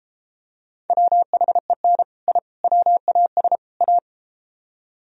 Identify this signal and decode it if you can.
Morse code, heard as a keyed tone, about 33 wpm, 700 Hz. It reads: WHEN I WAS A